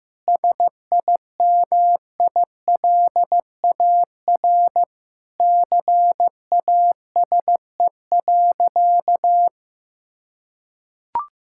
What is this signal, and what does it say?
Morse code, heard as a keyed tone, about 15 wpm, 700 Hz. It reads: SIMILAR CASE.
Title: Sentences from Top 1000 Words - Review Part 5 - 15wpm